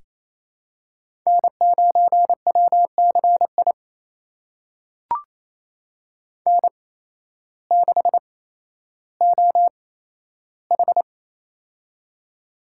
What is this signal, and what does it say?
Morse code, heard as a keyed tone, about 28 wpm, spaced as for 13 wpm, 700 Hz. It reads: N9WCI N6OH
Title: USA Call Signs - 8x Character Spacing - 28wpm